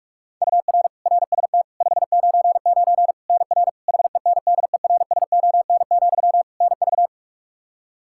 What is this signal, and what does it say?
Morse code, heard as a keyed tone, about 46 wpm, 700 Hz. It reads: UR RST 599 NR HENDERSON, NV
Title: Practice QSO - Exchange 1 - Beginning - 45wpm